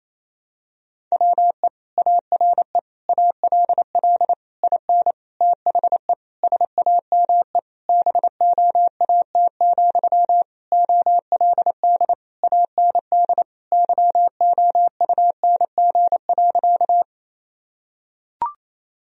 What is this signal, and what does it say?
Morse code, heard as a keyed tone, about 28 wpm, 700 Hz. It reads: WE ARE ALL IN THE SAME BOAT, OLD AND YOUNG.
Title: Sentences from Top 1000 Words - Review Part 3 - 28wpm